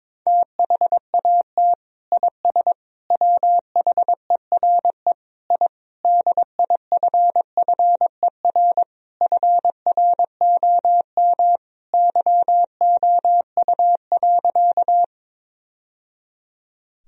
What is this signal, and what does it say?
Morse code, heard as a keyed tone, about 22 wpm, 700 Hz. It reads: THAT IS WHERE I DIFFER FROM YOU.
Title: Sentences from Top 1000 Words - Review Part 1 - 22wpm